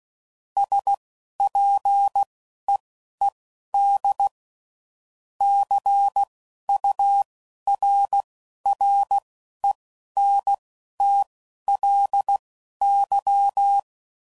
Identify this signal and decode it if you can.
Morse code, heard as a keyed tone, about 16 wpm, spaced as for 11 wpm, 800 Hz. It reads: SPEED CURRENTLY
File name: CW Code Groups @10WPM / Weather Sets / Weather 5 5